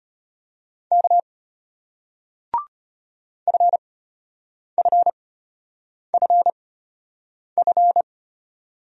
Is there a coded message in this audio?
Yes: Morse code from 0.9 to 8.0 s, heard as a keyed tone, about 34 wpm, spaced as for 8 wpm, 700 Hz. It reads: K FFFF